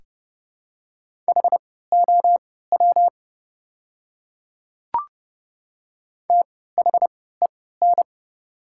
Morse code, heard as a keyed tone, about 30 wpm, spaced as for 17 wpm, 700 Hz. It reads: HOW THEN
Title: 3x Character Spacing - Top 100 Words - 30wpm